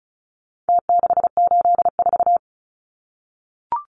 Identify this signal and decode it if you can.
Morse code, heard as a keyed tone, about 34 wpm, 700 Hz. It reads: T684
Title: Numbers - Contest Style - 4-Digit 35wpm